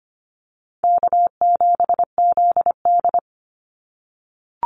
Morse code, heard as a keyed tone, about 25 wpm, 700 Hz. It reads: K7ZD
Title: Easy Call Signs - 25wpm